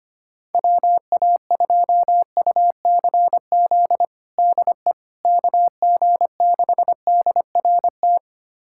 Morse code, heard as a keyed tone, about 25 wpm, 700 Hz. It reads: WA2UCZ DE KG6DRT
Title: USA Callsign DE Callsign - 6 Characters - 25wpm